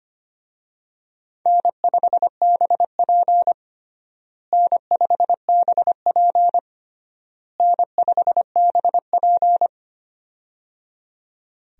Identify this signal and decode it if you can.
Morse code, heard as a keyed tone, about 25 wpm, 700 Hz. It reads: N5BP N5BP N5BP
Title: USA Call Signs - 4 Characters - Repeated 3x - 25wpm